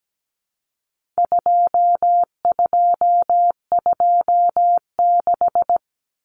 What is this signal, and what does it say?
Morse code, heard as a keyed tone, about 17 wpm, 700 Hz. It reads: 2226
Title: Common Cut Numbers - 4-Digit 17wpm